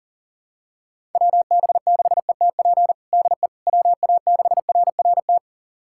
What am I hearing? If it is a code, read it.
Morse code, heard as a keyed tone, about 40 wpm, 700 Hz. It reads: WB6ETP DE WA6RRT
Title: USA Callsign DE Callsign - 6 Characters - 40wpm